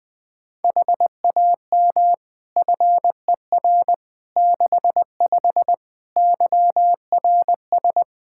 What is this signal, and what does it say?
Morse code, heard as a keyed tone, about 20 wpm, 700 Hz. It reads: HAM FER 65 YRS
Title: Practice QSO - Exchange 3 - More Info - 20wpm